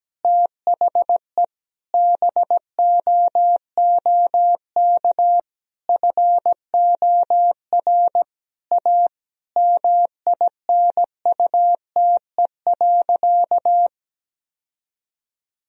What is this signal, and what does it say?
Morse code, heard as a keyed tone, about 17 wpm, 700 Hz. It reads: THE BOOK FOR A MINUTE.
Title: Sentences from Top 1000 Words - Review Part 2 - 17wpm